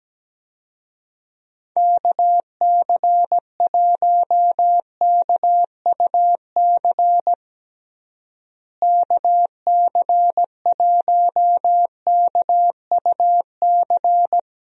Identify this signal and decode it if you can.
Morse code, heard as a keyed tone, about 17 wpm, 700 Hz. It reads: KC1KUC KC1KUC
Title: USA Call Signs - 6 Characters - Repeated 3x - 17wpm